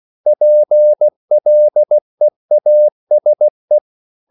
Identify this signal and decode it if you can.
Morse code, heard as a keyed tone, about 16 wpm, 600 Hz. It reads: PLEASE
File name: CW / Practice / EnglishQuotations0216